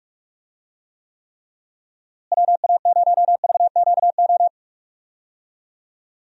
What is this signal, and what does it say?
Morse code, heard as a keyed tone, about 46 wpm, 700 Hz. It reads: WA0VQO
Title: USA Call Signs - 6 Characters - Repeated 3x - 45wpm